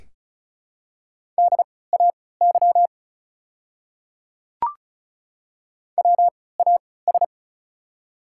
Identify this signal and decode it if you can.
Morse code, heard as a keyed tone, about 36 wpm, spaced as for 20 wpm, 690 Hz. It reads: DAY WAS
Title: 3x Character Spacing - Top 100 Words - 35wpm